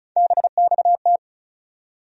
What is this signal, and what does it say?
Morse code, heard as a keyed tone, about 36 wpm, 700 Hz. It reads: BXT
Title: USA Call Sign Postfixes - 3 Letters 35wpm